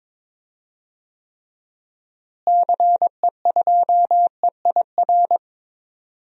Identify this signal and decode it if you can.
Morse code, heard as a keyed tone, about 21 wpm, 700 Hz. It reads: CE2EIR
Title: DXCC-100 Call Signs - 6 Characters - Repeated 3x - 22wpm